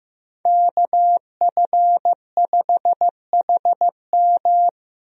Morse code, heard as a keyed tone, about 15 wpm, 700 Hz. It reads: KF5HM
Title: USA Call Signs - 5 Characters - Repeated 3x - 15wpm